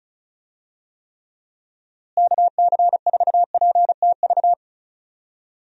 Morse code, heard as a keyed tone, about 34 wpm, 700 Hz. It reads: KC4PTV
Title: USA Call Signs - 6 Characters - Repeated 2x - 35wpm